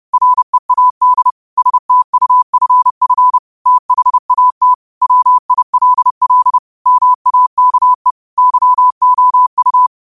Morse code, heard as a keyed tone, about 31 wpm, 1000 Hz. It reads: READ STUFF THAT WILL MAKE YOU
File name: qotdfiles / 30wpm